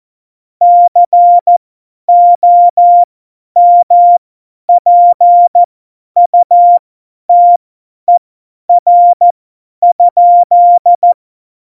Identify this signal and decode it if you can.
Morse code, heard as a keyed tone, about 14 wpm, spaced as for 10 wpm, 700 Hz. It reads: COMPUTER?
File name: Morsekurs / Witze / Witz 23. April 2019, 50 F70